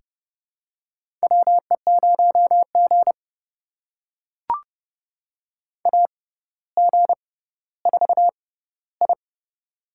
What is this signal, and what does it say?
Morse code, heard as a keyed tone, about 32 wpm, spaced as for 16 wpm, 700 Hz. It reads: WE0G AG4I
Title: USA Call Signs - 4 Characters - 6x Character Spacing - 30wpm